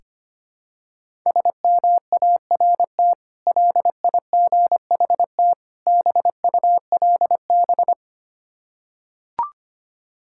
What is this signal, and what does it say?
Morse code, heard as a keyed tone, about 25 wpm, 700 Hz. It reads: SMART LIGHT BULB
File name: practice / 25wpm / Difficult Words with Hints - 25wpm